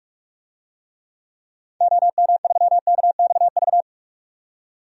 Morse code, heard as a keyed tone, about 46 wpm, 700 Hz. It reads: OM3KXV
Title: DXCC-100 Call Signs - 6 Characters - 45wpm